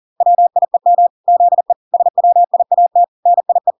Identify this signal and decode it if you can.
Morse code, heard as a keyed tone, about 40 wpm, 700 Hz. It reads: WIEM ZE SWIAT NIE